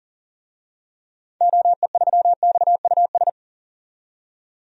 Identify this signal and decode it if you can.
Morse code, heard as a keyed tone, about 40 wpm, 700 Hz. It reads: OE3XUS
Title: DXCC-100 Call Signs - 6 Characters - 40wpm